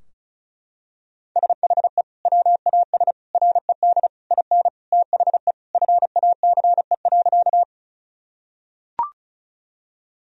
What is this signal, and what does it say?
Morse code, heard as a keyed tone, about 36 wpm, 700 Hz. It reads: SHE WAS RED IN THE FACE.
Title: Sentences from Top 1000 Words - Review Part 2 - 35wpm